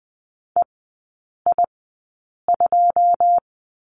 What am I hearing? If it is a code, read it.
Morse code, heard as a keyed tone, about 20 wpm, spaced as for 8 wpm, 690 Hz. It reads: EI2
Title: 081 Lesson 14e - Number 2 - Three Characters 20wpm